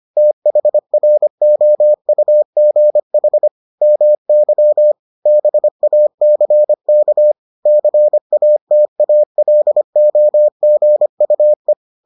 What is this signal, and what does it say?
Morse code, heard as a keyed tone, about 25 wpm, 600 Hz. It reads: THROUGH MY BACK CATALOGUE